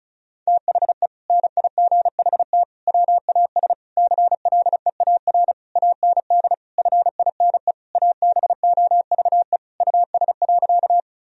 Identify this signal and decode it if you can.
Morse code, heard as a keyed tone, about 34 wpm, 700 Hz. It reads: THE NIGHT WAS CLEAR AND FINE ABOVE US.
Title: Sentences from Top 1000 Words - Review Part 2 - 35wpm